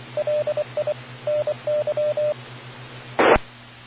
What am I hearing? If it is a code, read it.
Morse code, heard as a keyed tone, about 24 wpm, 590 Hz. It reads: LI NY I